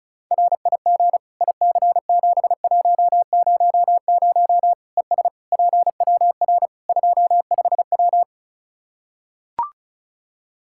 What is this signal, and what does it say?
Morse code, heard as a keyed tone, about 36 wpm, 700 Hz. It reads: RIG IC7100 ES PWR 25W
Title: K1USN SST QSO Practice 35wpm